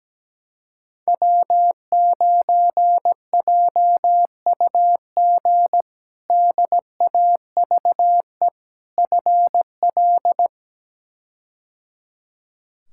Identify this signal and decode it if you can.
Morse code, heard as a keyed tone, about 17 wpm, 700 Hz. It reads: W9JUG DAVE FL
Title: K1USN SST QSO Practice - 17wpm